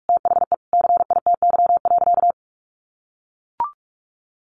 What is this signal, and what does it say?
Morse code, heard as a keyed tone, about 44 wpm, 700 Hz. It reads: THE CITY.
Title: Sentences from Top 1200 Words 45wpm